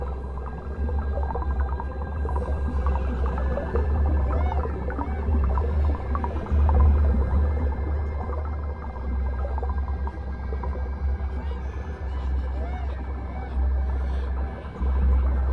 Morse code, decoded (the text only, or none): none